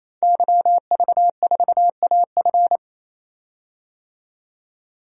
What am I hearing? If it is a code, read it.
Morse code, heard as a keyed tone, about 27 wpm, 700 Hz. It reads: YV4AF
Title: DXCC-100 Call Signs - 5 Characters - 28wpm